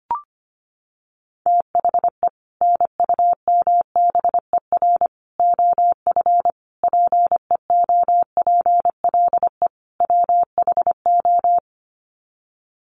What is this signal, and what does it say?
Morse code, heard as a keyed tone, about 25 wpm, 700 Hz. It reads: THE NUMBER OF PEOPLE WHO
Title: Sets of 5 Words Encore 25wpm